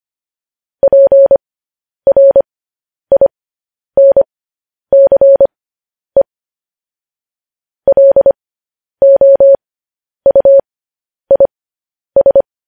Morse code, heard as a keyed tone, about 25 wpm, spaced as for 10 wpm, 550 Hz. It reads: PRINCE LOUIS
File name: podcast / News headlines-1x-25x10